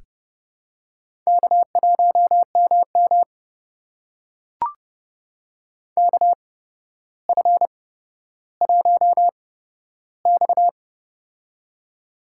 Morse code, heard as a keyed tone, about 28 wpm, spaced as for 13 wpm, 700 Hz. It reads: K1MM KF1X
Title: USA Call Signs - 4 Characters - 8x Character Spacing - 30wpm